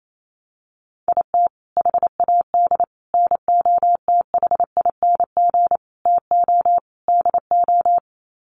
Morse code, heard as a keyed tone, about 28 wpm, 700 Hz. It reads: IT HAD NOTHING TO DO
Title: Sets of 5 Words Encore2 28wpm